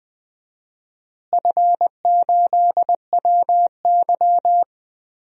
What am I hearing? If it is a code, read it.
Morse code, heard as a keyed tone, about 20 wpm, 700 Hz. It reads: F8WY
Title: DXCC-100 Call Signs - 4 Characters - Repeated 3x - 20wpm